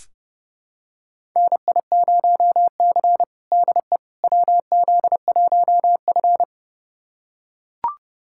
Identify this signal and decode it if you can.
Morse code, heard as a keyed tone, about 30 wpm, 700 Hz. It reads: NI0C DE WZ1F
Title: USA Callsign DE Callsign - 4 Characters - 30wpm